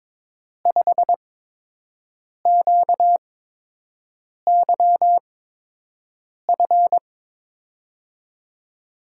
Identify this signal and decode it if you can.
Morse code, heard as a keyed tone, about 23 wpm, spaced as for 6 wpm, 700 Hz. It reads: 5QYF
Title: USA Call Signs - 6 Characters - 8x Character Spacing - 22wpm